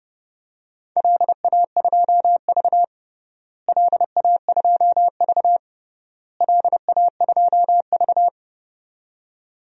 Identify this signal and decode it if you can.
Morse code, heard as a keyed tone, about 29 wpm, 700 Hz. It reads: LA2V LA2V LA2V
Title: DXCC-100 Call Signs - 4 Characters - Repeated 3x - 30wpm